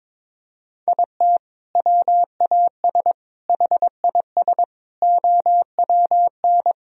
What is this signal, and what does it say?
Morse code, heard as a keyed tone, about 22 wpm, 700 Hz. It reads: IT WAS HIS OWN
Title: Sentences from Top 1000 Words - Review Part 3 - 22wpm